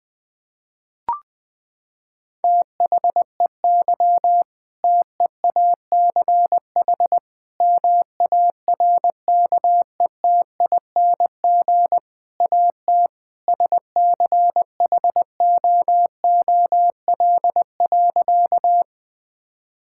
Morse code, heard as a keyed tone, about 20 wpm, 700 Hz. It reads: THEY TEACH MARKETING AT SCHOOL.
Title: Sentences from Top 1000 Words - Review Part 5 - 20wpm